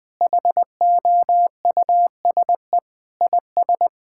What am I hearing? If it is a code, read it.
Morse code, heard as a keyed tone, about 20 wpm, 700 Hz. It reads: HOUSE IS
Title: Sentences with 3-Word Phrases - 20wpm